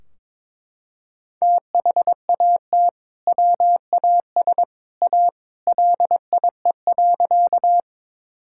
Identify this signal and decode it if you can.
Morse code, heard as a keyed tone, about 22 wpm, 700 Hz. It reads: THAT WAS A LIE.
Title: Sentences from Top 1600 Words 22wpm